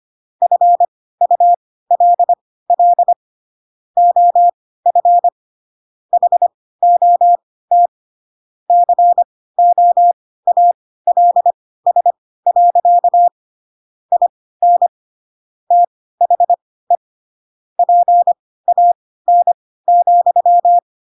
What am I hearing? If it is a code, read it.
Morse code, heard as a keyed tone, about 25 wpm, spaced as for 16 wpm, 700 Hz. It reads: FULL OF HOT COALS. IN THE PAN,